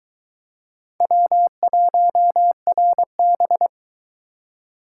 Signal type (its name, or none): Morse code